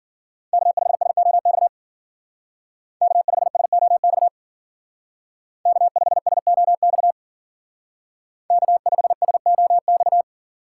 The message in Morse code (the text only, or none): K5SOX K5SOX K5SOX K5SOX